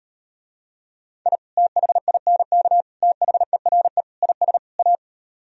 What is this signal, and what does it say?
Morse code, heard as a keyed tone, about 39 wpm, 700 Hz. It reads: I THINK THERE IS A